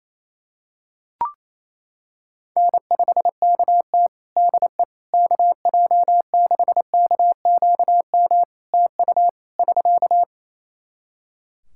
Morse code, heard as a keyed tone, about 28 wpm, 700 Hz. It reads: N5KT DE KJ6KQM TU <SK>
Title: Practice QSO - Exchange 4 - Ending - 28wpm